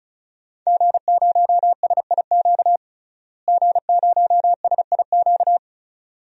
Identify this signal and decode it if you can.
Morse code, heard as a keyed tone, about 36 wpm, 700 Hz. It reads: G0SIQ G0SIQ
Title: DXCC-100 Call Signs - 5 Characters - Repeated 2x - 35wpm